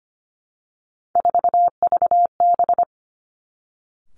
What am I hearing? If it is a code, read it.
Morse code, heard as a keyed tone, about 25 wpm, 700 Hz. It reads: 4VB
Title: Three Challenging Letter-Number Combinations 25wpm